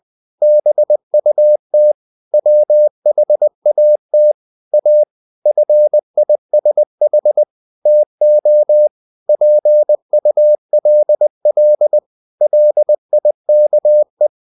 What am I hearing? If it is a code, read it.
Morse code, heard as a keyed tone, about 20 wpm, 600 Hz. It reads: BUT WHAT A FISH TO PULL LIKE
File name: the old man and the sea / the old man and the sea part 1 20wpm